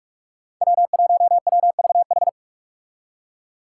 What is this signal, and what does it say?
Morse code, heard as a keyed tone, about 44 wpm, 700 Hz. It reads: W1WVH
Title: Five Challenging Letter-Number Combinations 45wpm